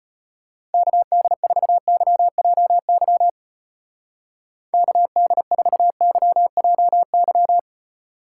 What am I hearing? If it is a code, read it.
Morse code, heard as a keyed tone, about 31 wpm, 700 Hz. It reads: KD4YJY KD4YJY
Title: Speed Racing - USA Call Signs 25wpm